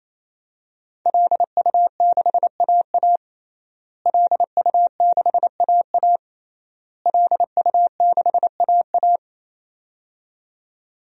Morse code, heard as a keyed tone, about 27 wpm, 700 Hz. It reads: LU6AA LU6AA LU6AA
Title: DXCC-100 Call Signs - 5 Characters - Repeated 3x - 28wpm